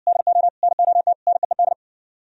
Morse code, heard as a keyed tone, about 70 wpm, 700 Hz. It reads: DO NOT NEED